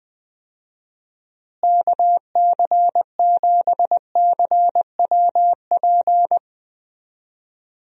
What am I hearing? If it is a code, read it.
Morse code, heard as a keyed tone, about 20 wpm, 700 Hz. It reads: KC7CWP
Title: USA Call Signs - 6 Characters - Repeated 3x - 20wpm